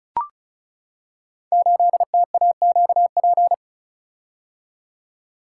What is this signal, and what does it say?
Morse code, heard as a keyed tone, about 34 wpm, 690 Hz. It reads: 8TAQP
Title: Code Groups - 5 Letter:Numbers - 35wpm